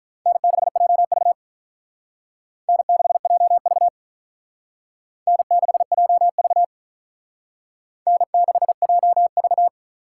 Morse code, heard as a keyed tone, about 44 wpm, 700 Hz. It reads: N6JV N6JV N6JV N6JV